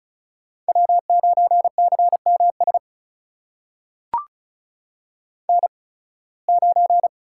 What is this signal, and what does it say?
Morse code, heard as a keyed tone, about 36 wpm, spaced as for 23 wpm, 700 Hz. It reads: W9CMS N9
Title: USA Call Signs - 5 Characters - 8x Character Spacing - 35wpm